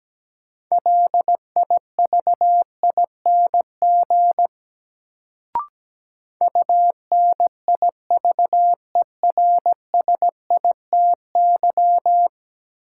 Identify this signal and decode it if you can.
Morse code, heard as a keyed tone, about 17 wpm, 700 Hz. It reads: LIVING UNIVERSITY
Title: Sets of 4 Words - New Word Drill - 17wpm